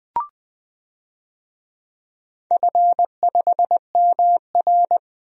Morse code, heard as a keyed tone, about 21 wpm, 700 Hz. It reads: F5MR
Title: DXCC-100 Call Signs - 4 Characters - Repeated 3x - 20wpm